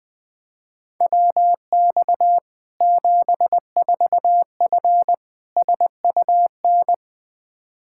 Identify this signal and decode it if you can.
Morse code, heard as a keyed tone, about 20 wpm, 700 Hz. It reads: WX 74F SUN
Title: Practice QSO - Exchange 2 - Info - 20wpm